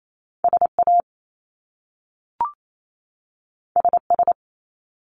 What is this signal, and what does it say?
Morse code, heard as a keyed tone, about 28 wpm, 700 Hz. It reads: SA SS
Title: Top Two Letter Combinations in Words 28wpm